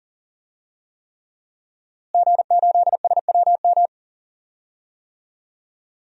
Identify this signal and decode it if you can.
Morse code, heard as a keyed tone, about 40 wpm, 700 Hz. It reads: G8SWM
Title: DXCC-100 Call Signs - 5 Characters - Repeated 3x - 40wpm